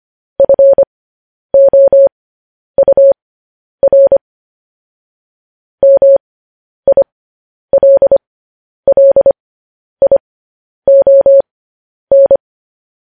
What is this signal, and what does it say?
Morse code, heard as a keyed tone, about 25 wpm, spaced as for 10 wpm, 550 Hz. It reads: FOUR MILLION
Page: news headlines in morse code at 10 wpm